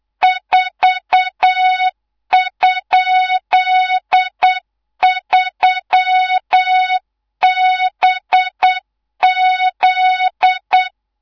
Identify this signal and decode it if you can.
Morse code, heard as a keyed tone, about 8 wpm, 3700 Hz, with noise. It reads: 4?3BZ
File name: Morze-HA2MN / hangp / 2 03 GYAKORLAT